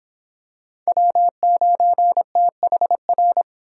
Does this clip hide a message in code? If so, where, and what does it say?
Yes: Morse code from 0.9 to 3.4 s, heard as a keyed tone, about 26 wpm, 700 Hz. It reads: W9THR